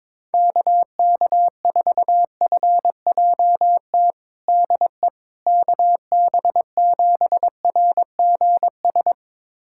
Morse code, heard as a keyed tone, about 22 wpm, 700 Hz. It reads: KK4FJT DE KB7RGS
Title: USA Callsign DE Callsign - 6 Characters - 22wpm